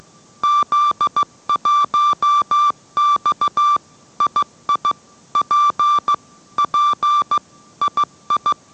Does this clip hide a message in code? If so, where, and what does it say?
Yes: Morse code from 0.4 to 8.5 s, heard as a keyed tone, about 15 wpm, 1170 Hz. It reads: Z1X II P P II